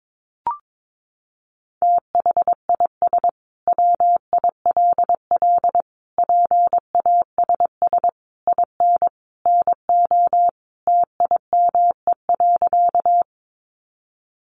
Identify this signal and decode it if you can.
Morse code, heard as a keyed tone, about 22 wpm, 700 Hz. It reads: THIS WILL PASS IN NO TIME.
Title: Sentences from Top 400 Words 22wpm